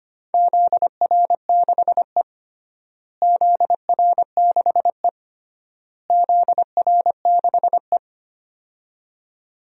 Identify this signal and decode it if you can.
Morse code, heard as a keyed tone, about 25 wpm, 700 Hz. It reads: ZR6E ZR6E ZR6E
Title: DXCC-100 Call Signs - 4 Characters - Repeated 3x - 25wpm